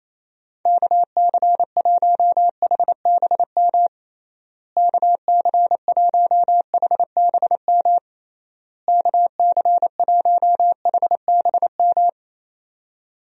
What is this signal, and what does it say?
Morse code, heard as a keyed tone, about 27 wpm, 700 Hz. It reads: KC1HBM KC1HBM KC1HBM
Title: USA Call Signs - 6 Characters - Repeated 3x - 28wpm